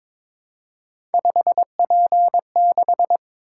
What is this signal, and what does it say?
Morse code, heard as a keyed tone, about 22 wpm, 700 Hz. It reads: 5P6
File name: practice / 22wpm / Three Letter-Number Combinations - 22wpm